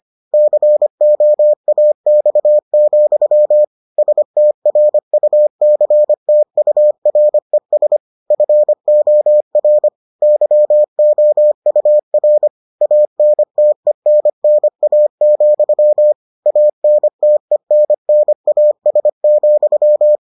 Morse code, heard as a keyed tone, about 25 wpm, 600 Hz. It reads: COAX, STRUCTURES FOR YOUR ANTENNA, ANTENNAS,